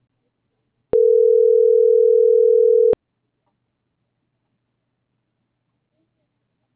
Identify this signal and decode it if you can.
Ringback tone (North America), 440 + 480 Hz.
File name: CENTRAL TERRESTRE / out karis - .5500